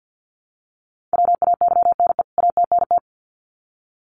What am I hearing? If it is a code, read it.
Morse code, heard as a keyed tone, about 50 wpm, 700 Hz. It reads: WAYNE ATNT
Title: CWT Exchange 50wpm